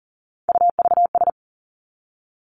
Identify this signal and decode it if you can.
Morse code, heard as a keyed tone, about 38 wpm, 700 Hz. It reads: UVS
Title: Three Challenging Letter-Number Combinations 40wpm